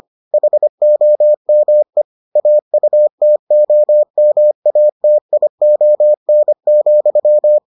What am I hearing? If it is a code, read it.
Morse code, heard as a keyed tone, about 25 wpm, 600 Hz. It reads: HOME AUTOMATION,